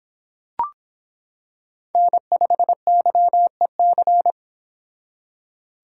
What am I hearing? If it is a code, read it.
Morse code, heard as a keyed tone, about 25 wpm, 700 Hz. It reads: N5YEC